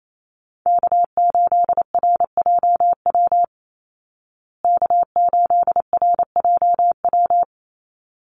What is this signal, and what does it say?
Morse code, heard as a keyed tone, about 28 wpm, 700 Hz. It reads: K8RJW K8RJW
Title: Call Signs Repeated 3x - 28wpm